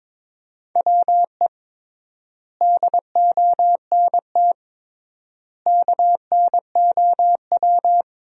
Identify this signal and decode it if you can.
Morse code, heard as a keyed tone, about 21 wpm, 700 Hz. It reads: WE DONT KNOW
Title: 3x Word-Spacing - Sentences from Top 400 Words - 22wpm